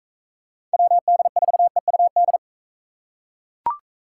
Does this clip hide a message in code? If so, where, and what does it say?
Yes: Morse code from 0.7 to 2.4 s, heard as a keyed tone, about 43 wpm, 700 Hz. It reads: WD4EUD